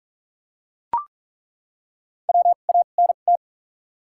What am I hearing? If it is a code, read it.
Morse code, heard as a keyed tone, about 42 wpm, spaced as for 31 wpm, 700 Hz. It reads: WANT
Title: 2x Character Spacing - Top 300 Words - 45wpm